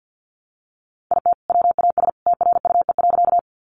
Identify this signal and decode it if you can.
Morse code, heard as a keyed tone, about 50 wpm, 700 Hz. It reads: IT WAS TRUE.
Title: Sentences from Top 400 Words 50wpm